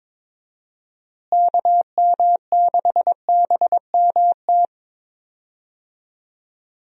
Morse code, heard as a keyed tone, about 22 wpm, 700 Hz. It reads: KM6BMT
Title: USA Call Signs - 6 Characters - Repeated 3x - 22wpm